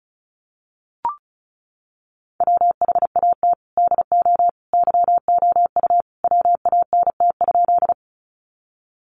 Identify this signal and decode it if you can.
Morse code, heard as a keyed tone, about 34 wpm, 700 Hz. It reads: WHAT DO YOU WANT?
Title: Sentences from Top 200 Words 35wpm